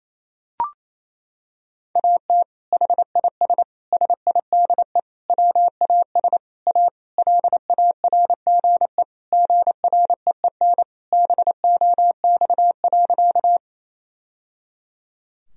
Morse code, heard as a keyed tone, about 28 wpm, 700 Hz. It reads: AT HIS SIDE WAS A LARGE GREEN BOX.
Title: Sentences from Top 1000 Words - Review Part 2 - 28wpm